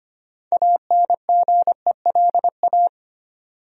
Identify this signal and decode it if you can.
Morse code, heard as a keyed tone, about 25 wpm, 700 Hz. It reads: ANGELA
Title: USA Common Names - 25wpm